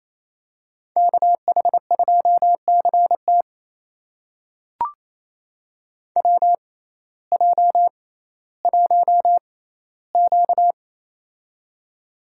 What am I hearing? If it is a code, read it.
Morse code, heard as a keyed tone, about 28 wpm, spaced as for 15 wpm, 700 Hz. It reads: KH2CT WJ1Q
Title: USA Call Signs - 6x Character Spacing - 28wpm